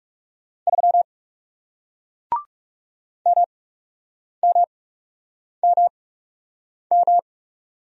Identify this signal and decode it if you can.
Morse code, heard as a keyed tone, about 40 wpm, spaced as for 8 wpm, 700 Hz. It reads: 3 MMMM